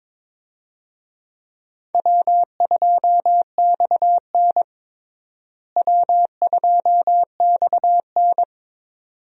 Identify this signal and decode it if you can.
Morse code, heard as a keyed tone, about 22 wpm, 700 Hz. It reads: W2XN W2XN
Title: USA Call Signs - 4 Characters - Repeated 2x - 22wpm